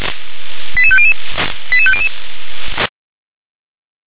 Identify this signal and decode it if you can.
Five-tone selective call (ZVEI), heard as quick runs of five tones, tones of about 70 ms. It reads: 88500 88500
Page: /357/SDR/ZVEI/